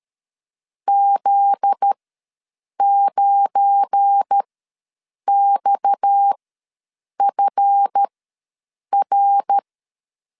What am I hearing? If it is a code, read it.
Morse code, heard as a keyed tone, about 13 wpm, spaced as for 7 wpm, 800 Hz. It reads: Z9XFR
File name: LEZIONI / 8 YJ45 / 8 YJ4 ws 12wpm 3249